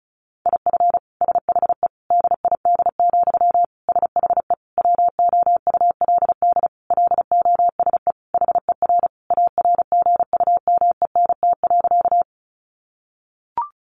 Morse code, heard as a keyed tone, about 34 wpm, 700 Hz. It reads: IF SHE DID, SHE WOULD LOSE HER ARGUMENT.
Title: Sentences from Top 1400 Words 35wpm